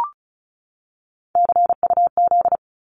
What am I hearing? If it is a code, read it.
Morse code, heard as a keyed tone, about 34 wpm, 700 Hz. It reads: CUZ
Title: Abbreviations 35wpm